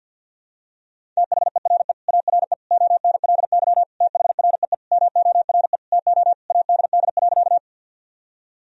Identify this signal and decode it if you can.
Morse code, heard as a keyed tone, about 51 wpm, 700 Hz. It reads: THERE ARE ONLY THREE MORE TO ADD.
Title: Sentences from Top 1000 Words - Review Part 1 - 50wpm